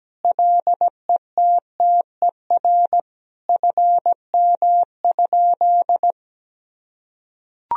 Morse code, heard as a keyed tone, about 17 wpm, 700 Hz. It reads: LETTER FM?
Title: Sentences from Top 300 Words and Abbreviations 17wpm